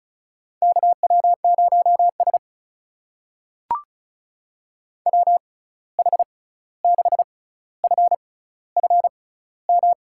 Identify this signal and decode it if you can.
Morse code, heard as a keyed tone, about 36 wpm, spaced as for 16 wpm, 700 Hz. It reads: KW0S WH6FFM
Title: USA Call Signs - 6x Character Spacing - 35wpm